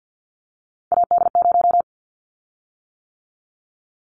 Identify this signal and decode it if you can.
Morse code, heard as a keyed tone, about 50 wpm, 700 Hz. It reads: AD0
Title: Callsign Prefixes for Top 25 DXCC - 50wpm